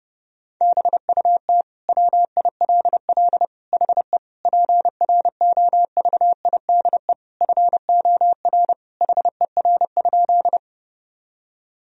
Morse code, heard as a keyed tone, about 30 wpm, 700 Hz. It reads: BUT WILL HE PROVIDE FOR HER?
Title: Sentences from Top 1000 Words - Review Part 3 - 30wpm